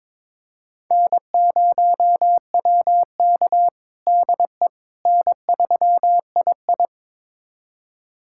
Morse code, heard as a keyed tone, about 22 wpm, 700 Hz. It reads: N0WK DE N3II
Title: USA Callsign DE Callsign - 4 Characters - 22wpm